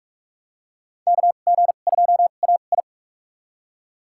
Morse code, heard as a keyed tone, about 43 wpm, spaced as for 32 wpm, 700 Hz. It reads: KG2AI
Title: USA Call Signs - 5 Characters - 2x Character Spacing - 45wpm